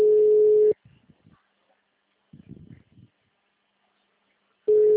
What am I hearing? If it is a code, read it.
Ringback tone (Europe, 425 Hz).